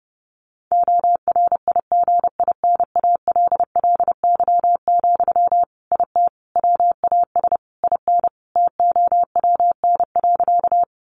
Morse code, heard as a keyed tone, about 30 wpm, 700 Hz. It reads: ORIGINALLY, IT WAS IN TOWN.
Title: Sentences from Top 1500 Words 30wpm